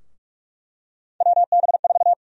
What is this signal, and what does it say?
Morse code, heard as a keyed tone, about 46 wpm, 700 Hz. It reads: WB4
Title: USA Call Sign Prefixes - Repeated 2x - 45wpm